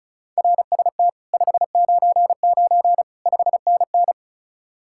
Morse code, heard as a keyed tone, about 43 wpm, 700 Hz. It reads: RST 599 5NN
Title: Speed-Racing Common QSO Phrases - 35wpm